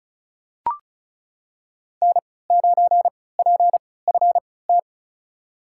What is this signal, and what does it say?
Morse code, heard as a keyed tone, about 36 wpm, spaced as for 20 wpm, 700 Hz. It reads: N9PFT
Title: USA Call Signs - 3x Character Spacing - 35wpm